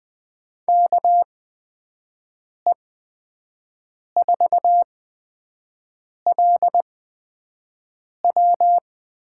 Morse code, heard as a keyed tone, about 20 wpm, spaced as for 5 wpm, 700 Hz. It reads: KE4LW
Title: USA Call Signs - 8x Character Spacing - 20wpm